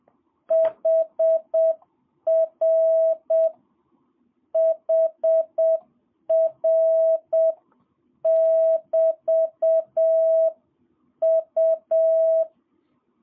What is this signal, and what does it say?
Morse code, heard as a keyed tone, about 7 wpm, 650 Hz. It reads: HR HR <BT> U
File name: jh8cbh / 2022cw